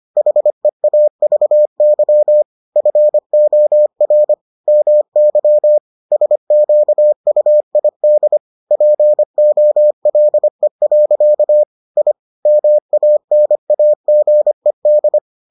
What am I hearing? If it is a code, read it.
Morse code, heard as a keyed tone, about 25 wpm, 600 Hz. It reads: HEAVY FOR MY SQUID POLE. I MANAGED